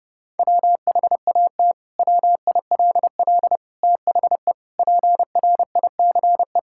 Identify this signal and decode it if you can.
Morse code, heard as a keyed tone, about 32 wpm, 700 Hz. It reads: WHAT WILL THE PRICE
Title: Sentences from Top 1000 Words - Review Part 5 - 30wpm